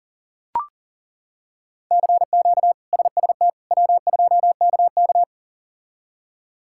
Morse code, heard as a keyed tone, about 40 wpm, 700 Hz. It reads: CQ SST W2KK
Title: K1USN SST QSO Practice - 40wpm